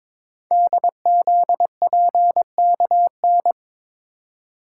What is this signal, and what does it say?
Morse code, heard as a keyed tone, about 22 wpm, 700 Hz. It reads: DZPKN
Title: Code Groups - 5 Letters - 22wpm